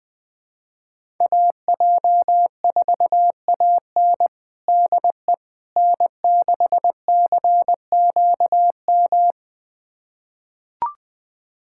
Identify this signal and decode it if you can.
Morse code, heard as a keyed tone, about 20 wpm, 700 Hz. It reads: AJ4AN DE N6CQM
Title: USA Callsign DE Callsign - 5 Characters - 20wpm